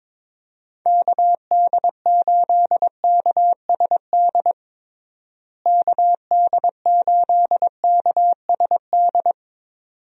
Morse code, heard as a keyed tone, about 22 wpm, 700 Hz. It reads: KD8KSD KD8KSD
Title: USA Call Signs - 6 Characters - Repeated 2x - 22wpm